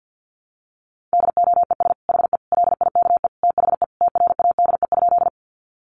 Morse code, heard as a keyed tone, about 50 wpm, 700 Hz. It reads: DOES HE LIKE THE TRADE?
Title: Sentences from Top 600 Words 50wpm